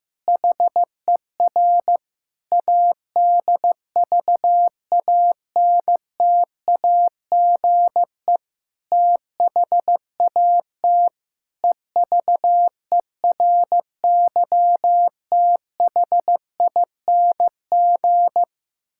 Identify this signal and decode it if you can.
Morse code, heard as a keyed tone, about 15 wpm, 700 Hz. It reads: HER ADVANTAGE THAT EVERYTHING
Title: Sentences from Top 1000 Words - Review Part 6 - 15wpm